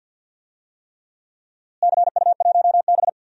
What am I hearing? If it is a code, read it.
Morse code, heard as a keyed tone, about 52 wpm, 700 Hz. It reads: CU1B
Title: DXCC-100 Call Signs - 4 Characters - Repeated 3x - 50wpm